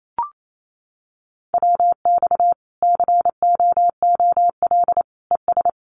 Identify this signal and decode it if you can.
Morse code, heard as a keyed tone, about 28 wpm, 700 Hz. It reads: WX COOL ES
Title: Common QSO Phrases 28wpm